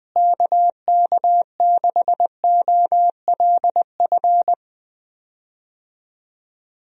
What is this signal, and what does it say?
Morse code, heard as a keyed tone, about 20 wpm, 700 Hz. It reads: KK6OLF
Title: USA Call Signs - 6 Characters - Repeated 3x - 20wpm